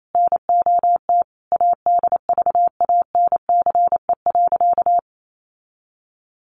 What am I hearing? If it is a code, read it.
Morse code, heard as a keyed tone, about 28 wpm, 700 Hz. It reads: NOT ADVANCE.
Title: Sentences from Top 1400 Words 28wpm